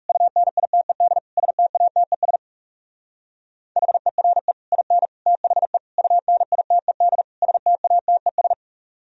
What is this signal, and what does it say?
Morse code, heard as a keyed tone, about 35 wpm, 700 Hz. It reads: UNITED STATES HERE IN THE UNITED STATES
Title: Speed-Racing - Sets of 5 Words 30wpm